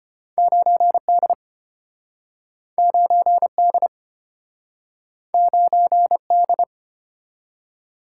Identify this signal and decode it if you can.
Morse code, heard as a keyed tone, about 30 wpm, 700 Hz. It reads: 9D 9D 9D
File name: practice / 25wpm / Speed-Racing - Two Letter-Number Combinations - 25wpm